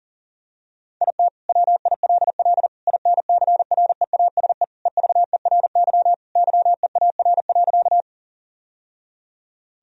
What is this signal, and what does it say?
Morse code, heard as a keyed tone, about 40 wpm, 700 Hz. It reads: IT WILL INCREASE EVERY YEAR.
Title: Sentences from Top 1000 Words - Review Part 4 - 40wpm